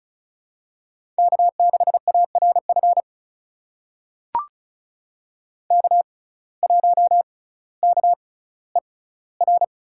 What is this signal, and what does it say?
Morse code, heard as a keyed tone, about 36 wpm, spaced as for 18 wpm, 700 Hz. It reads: K6ARF K1KER
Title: USA Call Signs - 5 Characters - 6x Character Spacing - 35wpm